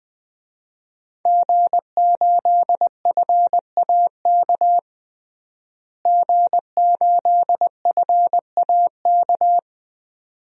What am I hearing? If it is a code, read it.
Morse code, heard as a keyed tone, about 20 wpm, 700 Hz. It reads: G8FAK G8FAK
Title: DXCC-100 Call Signs - 5 Characters - Repeated 2x - 20wpm